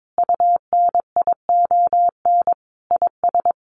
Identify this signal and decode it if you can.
Morse code, heard as a keyed tone, about 22 wpm, 700 Hz. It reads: UNION IS
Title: Sentences from Top 1400 Words 22wpm